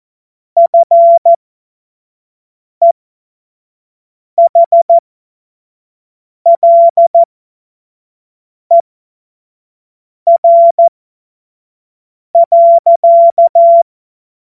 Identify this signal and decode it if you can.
Morse code, heard as a keyed tone, about 14 wpm, spaced as for 5 wpm, 680 Hz. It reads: FEHLER.
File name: Morsekurs / Witze / Witz 27. Januar 2026 25 F70